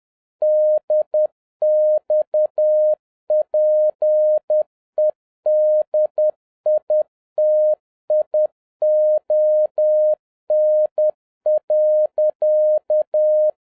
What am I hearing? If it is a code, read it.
Morse code, heard as a keyed tone, about 10 wpm, 610 Hz. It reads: DXPEDITION.
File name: 2011-07-mp3 / n56 BACK FROM A BIG ACCIDENT 10wpm